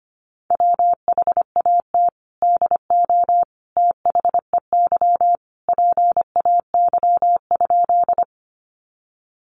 Morse code, heard as a keyed tone, about 25 wpm, 700 Hz. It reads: WHAT DO THEY PAY?
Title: Sentences from Top 700 Words 25wpm